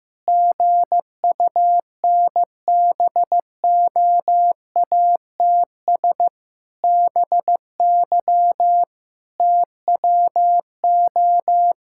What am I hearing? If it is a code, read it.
Morse code, heard as a keyed tone, about 15 wpm, 700 Hz. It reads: GUNBOATS BY TWO
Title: Pangram Challenge v2 - 15wpm